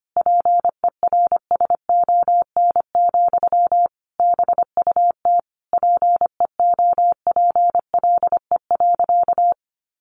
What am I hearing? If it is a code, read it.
Morse code, heard as a keyed tone, about 25 wpm, 700 Hz. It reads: PERSON, BUT PEOPLE.
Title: Sentences from Top 1400 Words 25wpm